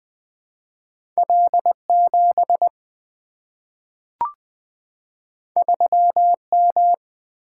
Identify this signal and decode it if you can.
Morse code, heard as a keyed tone, about 20 wpm, 700 Hz. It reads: L7 3M
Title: Two Letter-Number Combinations - 20wpm